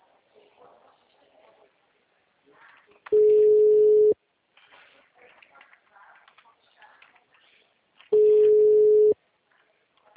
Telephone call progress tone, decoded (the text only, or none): ringback tone